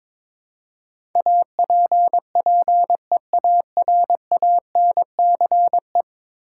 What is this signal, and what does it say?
Morse code, heard as a keyed tone, about 22 wpm, 700 Hz. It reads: APPEARANCE
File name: practice / 22wpm / 10 Letter Words - First 3 Letters Given - 22wpm